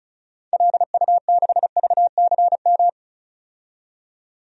Morse code, heard as a keyed tone, about 36 wpm, 700 Hz. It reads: LU6VCM
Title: DXCC-100 Call Signs - 6 Characters - 35wpm